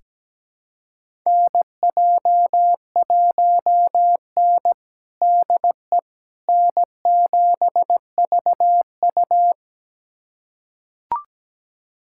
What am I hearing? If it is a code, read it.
Morse code, heard as a keyed tone, about 17 wpm, 700 Hz. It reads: NJ1N DE N7VU
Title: USA Callsign DE Callsign - 4 Characters - 17wpm